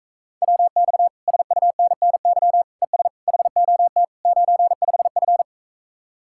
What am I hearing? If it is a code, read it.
Morse code, heard as a keyed tone, about 44 wpm, 700 Hz. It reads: WX SUNNY ES HOT 95F